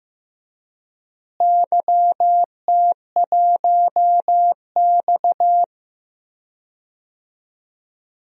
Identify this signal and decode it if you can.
Morse code, heard as a keyed tone, about 16 wpm, 700 Hz. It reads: YT1X
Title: DXCC-100 Call Signs - 4 Characters - Repeated 3x - 15wpm